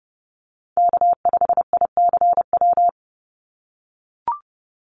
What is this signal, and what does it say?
Morse code, heard as a keyed tone, about 30 wpm, 700 Hz. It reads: K5ICW
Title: USA Call Signs 30wpm